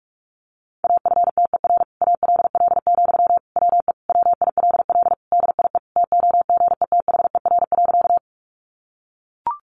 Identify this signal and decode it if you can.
Morse code, heard as a keyed tone, about 44 wpm, 700 Hz. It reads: AFTER ALL, WE WILL DIE TOGETHER.
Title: Sentences from Top 1300 Words 45wpm